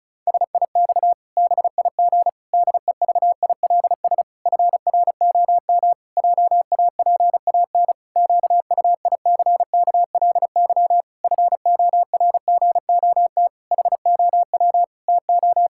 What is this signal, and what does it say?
Morse code, heard as a keyed tone, about 36 wpm, 700 Hz. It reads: SIX BIG DEVILS FROM JAPAN QUICKLY FORGOT HOW TO
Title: Pangram Challenge v2 - 35wpm